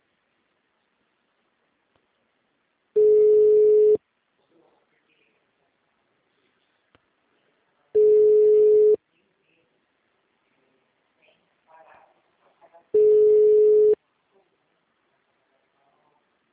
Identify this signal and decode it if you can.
Ringback tone (Europe, 425 Hz), with speech.